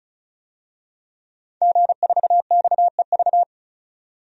Morse code, heard as a keyed tone, about 35 wpm, 700 Hz. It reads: G4XEV
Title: DXCC-100 Call Signs - 5 Characters - 35wpm